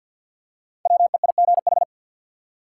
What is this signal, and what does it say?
Morse code, heard as a keyed tone, about 53 wpm, 700 Hz. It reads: WEIGH
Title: Five Letter Words - 50wpm